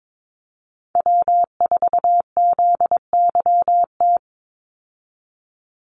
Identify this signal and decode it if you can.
Morse code, heard as a keyed tone, about 22 wpm, 700 Hz. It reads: W4ZYT
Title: Call Signs - 22wpm